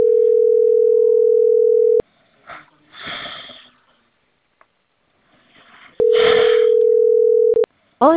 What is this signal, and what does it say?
Ringback tone (North America), 440 + 480 Hz, with speech.
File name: CENTRAL CAEA / out agcaea02 - .21544